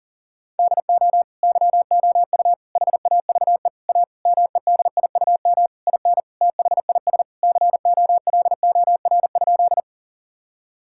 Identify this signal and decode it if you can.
Morse code, heard as a keyed tone, about 40 wpm, 700 Hz. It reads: DO YOU HAVE A MEDIUM IN THIS COLOR?
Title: Sentences from Top 1000 Words - Review Part 6 - 40wpm